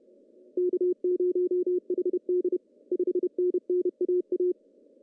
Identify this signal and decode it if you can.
Morse code, heard as a keyed tone, about 31 wpm, 370 Hz. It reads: K0HD 5NN AA
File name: CW / cqww-cw-k5tr